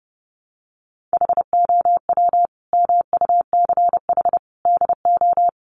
Morse code, heard as a keyed tone, about 30 wpm, 700 Hz. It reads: HOW MUCH DO
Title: Sentences from Top 200 Words 30wpm